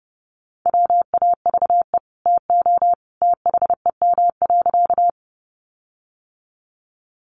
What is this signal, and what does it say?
Morse code, heard as a keyed tone, about 30 wpm, 700 Hz. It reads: WAVE TO THEM.
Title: Sentences from Top 1400 Words 30wpm